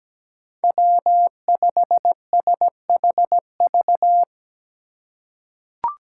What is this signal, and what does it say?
Morse code, heard as a keyed tone, about 17 wpm, 700 Hz. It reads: W5SHV
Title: USA Call Signs - 2x Character Spacing - 17wpm